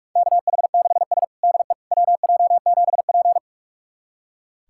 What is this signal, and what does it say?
Morse code, heard as a keyed tone, about 41 wpm, 700 Hz. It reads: KH6S DE WJ7P
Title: USA Callsign DE Callsign - 4 Characters - 45wpm